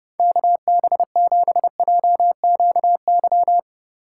Morse code, heard as a keyed tone, about 32 wpm, 700 Hz. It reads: KB7JQY